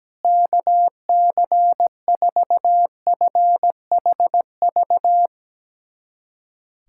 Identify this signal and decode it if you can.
Morse code, heard as a keyed tone, about 17 wpm, 700 Hz. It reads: KC4FHV
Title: USA Callsign DE Callsign - 6 Characters - 17wpm